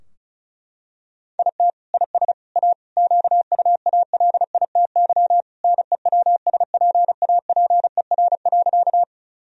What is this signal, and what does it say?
Morse code, heard as a keyed tone, about 34 wpm, 700 Hz. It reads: IT IS A QUALITY NEWSPAPER.
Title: Sentences from Top 1000 Words - Review Part 4 - 35wpm